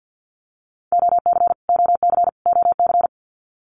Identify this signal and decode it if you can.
Morse code, heard as a keyed tone, about 50 wpm, 700 Hz. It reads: OC OC OC
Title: Abbreviations Repeated 3x - 50wpm